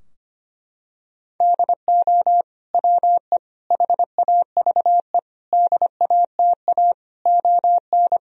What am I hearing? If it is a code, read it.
Morse code, heard as a keyed tone, about 25 wpm, 700 Hz. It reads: DO WE HAVE DATA ON
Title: Sentences from Top 1000 Words - Review Part 3 - 25wpm